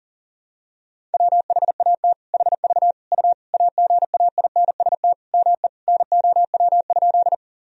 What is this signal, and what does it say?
Morse code, heard as a keyed tone, about 40 wpm, 700 Hz. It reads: WHAT HV U AGAINST ME NOW?
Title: Sentences from Top 400 Words and Abbreviations 40wpm